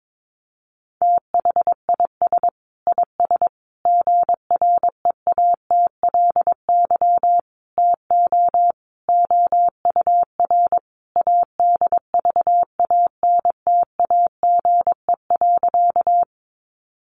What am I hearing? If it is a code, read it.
Morse code, heard as a keyed tone, about 22 wpm, 700 Hz. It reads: THIS IS GREATLY TO OUR ADVANTAGE.
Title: Sentences from Top 1400 Words 22wpm